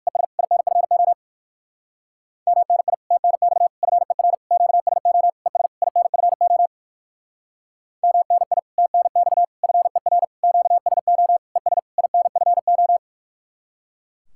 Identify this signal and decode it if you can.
Morse code, heard as a keyed tone, about 56 wpm, 700 Hz. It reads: ES INFO MNI TNX FER QSO ES INFO MNI TNX FER QSO ES INFO
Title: Speed-Racing Common QSO Phrases Encore - 45wpm